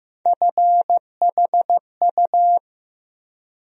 Morse code, heard as a keyed tone, about 16 wpm, 700 Hz. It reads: FHU
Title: USA Call Sign Postfixes - 3 Letters 15wpm